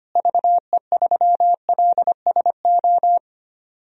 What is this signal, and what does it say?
Morse code, heard as a keyed tone, about 25 wpm, 700 Hz. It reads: VE3LSO
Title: DXCC-100 Call Signs - 6 Characters - Repeated 2x - 25wpm